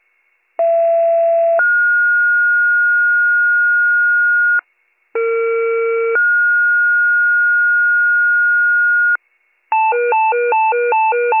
Two-tone page, heard as a one-second tone, then a three-second tone, about 670 then 1430 Hz, about 460 then 1430 Hz. It is heard from 0.6 to 9.2 s.